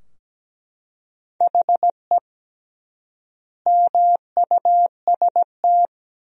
Morse code, heard as a keyed tone, about 17 wpm, 700 Hz. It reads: HE MUST
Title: 3x Word-Spacing - Sentences from Top 500 Words - 17wpm